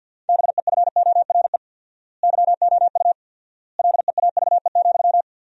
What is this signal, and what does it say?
Morse code, heard as a keyed tone, about 50 wpm, 700 Hz. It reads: BEFORE YOU LEAVE,
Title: 4x Word-Spacing - Sentences from Top 400 Words - 50wpm